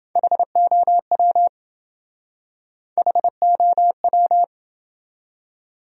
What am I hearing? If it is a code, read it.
Morse code, heard as a keyed tone, about 27 wpm, 700 Hz. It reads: HOW HOW